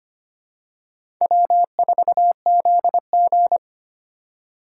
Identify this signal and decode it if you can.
Morse code, heard as a keyed tone, about 25 wpm, 700 Hz. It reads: W4ZG
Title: USA Call Signs - 3x Character Spacing - 25wpm